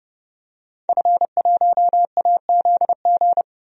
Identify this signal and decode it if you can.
Morse code, heard as a keyed tone, about 31 wpm, 700 Hz. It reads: F1AZG
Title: DXCC-100 Call Signs - 5 Characters - 30wpm